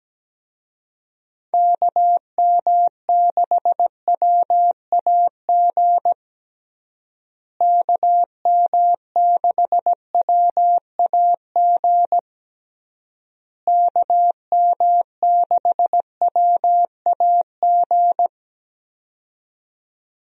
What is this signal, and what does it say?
Morse code, heard as a keyed tone, about 17 wpm, 700 Hz. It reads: KM6WAG KM6WAG KM6WAG
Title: USA Call Signs - 6 Characters - Repeated 3x - 17wpm